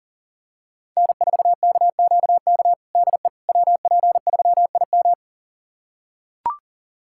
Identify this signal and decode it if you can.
Morse code, heard as a keyed tone, about 40 wpm, 700 Hz. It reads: N4KQK DE WP3IM
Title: USA Callsign DE Callsign - 5 Characters - 40wpm